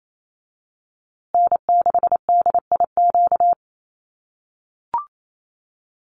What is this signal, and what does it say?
Morse code, heard as a keyed tone, about 28 wpm, 700 Hz. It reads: N6DIQ
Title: USA Call Signs Repeated 3x 28wpm